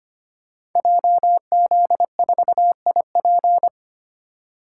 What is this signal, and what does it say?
Morse code, heard as a keyed tone, about 25 wpm, 700 Hz. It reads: JZ4IP
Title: Code Groups - 5 Letter:Numbers - 25wpm